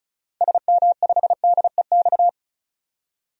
Morse code, heard as a keyed tone, about 36 wpm, 700 Hz. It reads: SM5DEX
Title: DXCC-100 Call Signs - 6 Characters - 35wpm